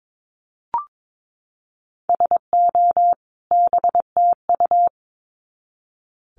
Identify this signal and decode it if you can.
Morse code, heard as a keyed tone, about 22 wpm, 700 Hz. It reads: SO BTU
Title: Common QSO Phrases 22wpm